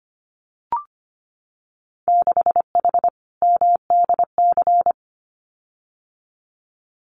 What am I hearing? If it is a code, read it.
Morse code, heard as a keyed tone, about 25 wpm, 700 Hz. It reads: 6H MDC
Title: Winter Field Day Exchanges Encore 25wpm